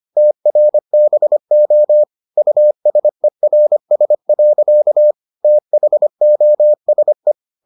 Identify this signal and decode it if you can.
Morse code, heard as a keyed tone, about 25 wpm, 600 Hz. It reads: TRBO USERS. THOSE